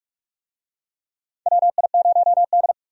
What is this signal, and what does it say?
Morse code, heard as a keyed tone, about 46 wpm, 700 Hz. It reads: WI0D
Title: USA Call Signs - 4 Characters - 45wpm